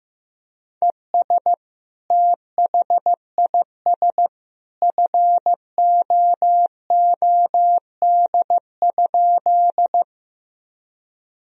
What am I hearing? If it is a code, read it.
Morse code, heard as a keyed tone, about 16 wpm, 700 Hz. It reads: ES THIS FOOD?
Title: Sentences from Top 300 Words and Abbreviations 15wpm